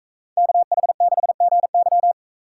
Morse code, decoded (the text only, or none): KH6GY